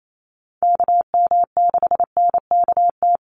Morse code, heard as a keyed tone, about 28 wpm, 700 Hz. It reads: KM6NKT
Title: USA Call Signs 28wpm